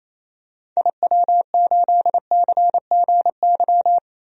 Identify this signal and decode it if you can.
Morse code, heard as a keyed tone, about 27 wpm, 700 Hz. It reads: IW8CGY
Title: DXCC-100 Call Signs - 6 Characters - Repeated 3x - 28wpm